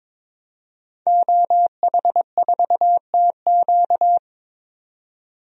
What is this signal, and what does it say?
Morse code, heard as a keyed tone, about 21 wpm, 700 Hz. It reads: OH4TQ
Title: DXCC-100 Call Signs - 5 Characters - 22wpm